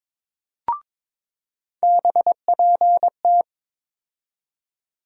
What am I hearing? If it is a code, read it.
Morse code, heard as a keyed tone, about 23 wpm, 700 Hz. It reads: BPT
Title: USA Call Sign Postfixes - 3 Letters 22wpm